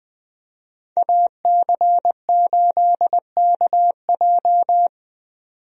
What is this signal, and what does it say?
Morse code, heard as a keyed tone, about 20 wpm, 700 Hz. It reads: AC8KJ
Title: USA Call Signs - 5 Characters - 20wpm